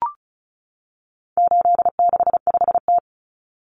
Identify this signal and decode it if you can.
Morse code, heard as a keyed tone, about 34 wpm, 700 Hz. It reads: 865T
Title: Common Cut Numbers - 4-Digit 35wpm